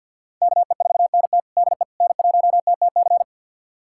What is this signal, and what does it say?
Morse code, heard as a keyed tone, about 49 wpm, 700 Hz. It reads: KE4NT DE N1TTC
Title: USA Callsign DE Callsign - 5 Characters - 50wpm